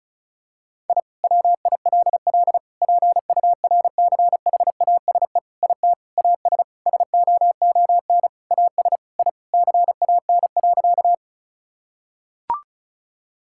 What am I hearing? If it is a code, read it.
Morse code, heard as a keyed tone, about 36 wpm, 700 Hz. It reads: I WILL PURCHASE IT AS SOON AS I CAN.
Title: Sentences from Top 1000 Words - Review Part 5 - 35wpm